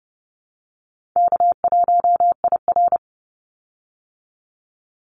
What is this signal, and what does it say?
Morse code, heard as a keyed tone, about 30 wpm, 700 Hz. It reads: K1IR
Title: Call Signs Repeated 3x - 30wpm